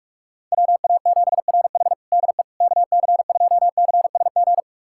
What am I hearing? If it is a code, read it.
Morse code, heard as a keyed tone, about 46 wpm, 700 Hz. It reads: WA7RH DE KC2CSG
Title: USA Callsign DE Callsign 45wpm